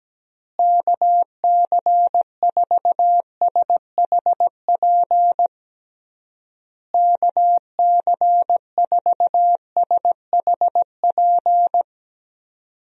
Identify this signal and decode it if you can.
Morse code, heard as a keyed tone, about 17 wpm, 700 Hz. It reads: KC4SHP KC4SHP
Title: USA Call Signs - 6 Characters - Repeated 3x - 17wpm